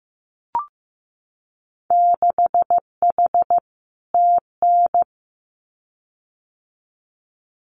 Morse code, heard as a keyed tone, about 15 wpm, 700 Hz. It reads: 6H TN
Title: Winter Field Day Exchanges Encore 15wpm